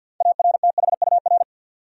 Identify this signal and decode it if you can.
Morse code, heard as a keyed tone, about 53 wpm, 700 Hz. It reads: ARTHUR
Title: USA Common Names - 50wpm